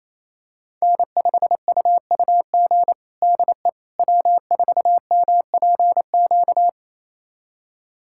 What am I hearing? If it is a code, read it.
Morse code, heard as a keyed tone, about 28 wpm, 700 Hz. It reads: N5UUG DE W4MPQ
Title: USA Callsign DE Callsign - 5 Characters - 28wpm